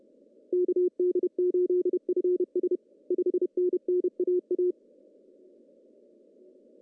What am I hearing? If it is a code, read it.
Morse code, heard as a keyed tone, about 31 wpm, 370 Hz. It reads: KD8FS 5NN AA
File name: CW / cqww-cw-k5tr